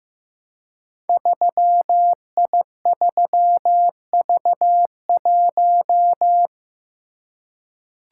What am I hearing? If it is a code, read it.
Morse code, heard as a keyed tone, about 15 wpm, 700 Hz. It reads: 3I3V1
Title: Code Groups - 5 Letter:Numbers - 15wpm